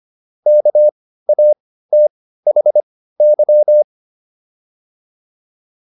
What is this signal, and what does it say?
Morse code, heard as a keyed tone, about 25 wpm, spaced as for 15 wpm, 600 Hz. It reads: KATHY